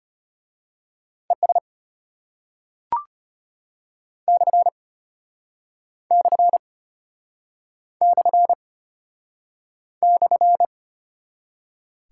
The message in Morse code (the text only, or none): ES / / / /